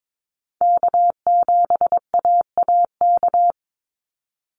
Morse code, heard as a keyed tone, about 22 wpm, 700 Hz. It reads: K7AAK
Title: USA Call Signs 22wpm